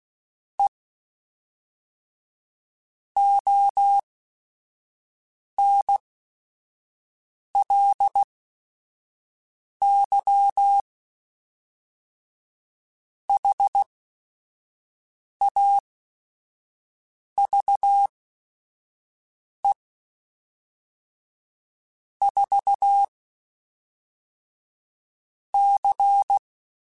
Morse code, heard as a keyed tone, about 16 wpm, spaced as for 5 wpm, 800 Hz. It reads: E ONLY HAVE 4 C
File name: Technical Subjects / Technical Subjects-mp3s / AND Gate @5